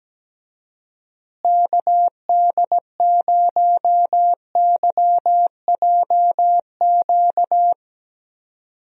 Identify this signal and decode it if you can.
Morse code, heard as a keyed tone, about 17 wpm, 700 Hz. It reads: KD0YJQ
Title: USA Call Signs - 6 Characters - Repeated 3x - 17wpm